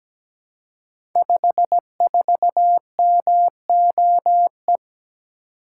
Morse code, heard as a keyed tone, about 17 wpm, 700 Hz. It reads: 54MOE
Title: Code Groups - 5 Letter:Numbers - 17wpm